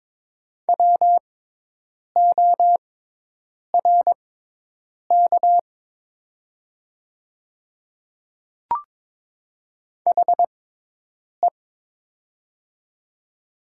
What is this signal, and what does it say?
Morse code, heard as a keyed tone, about 22 wpm, spaced as for 8 wpm, 700 Hz. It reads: WORK HE
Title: 6x Character Spacing - Top 200 Words - 22wpm